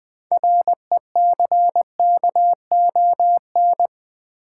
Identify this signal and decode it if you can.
Morse code, heard as a keyed tone, about 20 wpm, 700 Hz. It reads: RECKON
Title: Build A Word - 3 Letters At A Time - 6-Letter Words - 20wpm